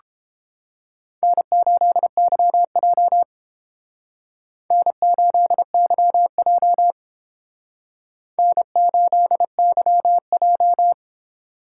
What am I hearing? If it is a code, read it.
Morse code, heard as a keyed tone, about 29 wpm, 700 Hz. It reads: N8YJ N8YJ N8YJ